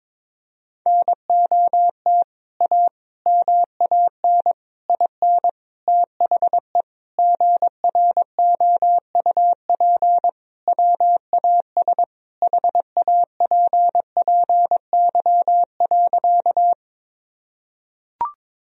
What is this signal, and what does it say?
Morse code, heard as a keyed tone, about 22 wpm, 700 Hz. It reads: NOT A MAN IN THE GROUP WAS HAPPY.
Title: Sentences from Top 1000 Words - Review Part 1 - 22wpm